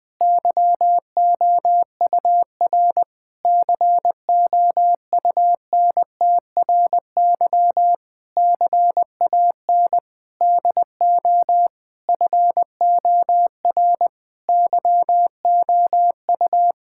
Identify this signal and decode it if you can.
Morse code, heard as a keyed tone, about 20 wpm, 700 Hz. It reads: YOUR COUNTRY CAN DO FOR YOU
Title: Sentences with 9-Word Phrases - 20wpm